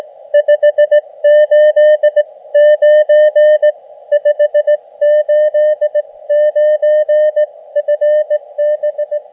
Morse code, heard as a keyed tone, about 19 wpm, spaced as for 16 wpm, 600 Hz. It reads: 589 589 FB
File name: Expedition / SWL UT5NM 20.12.2012 VANGUARD UA1CEG p IT9IFI